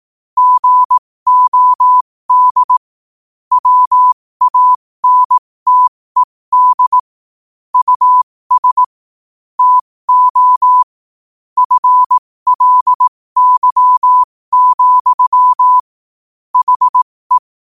Morse code, heard as a keyed tone, about 18 wpm, spaced as for 15 wpm, 1000 Hz. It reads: GOD WANTED US TO FLY, HE
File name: qotdfiles / 15wpm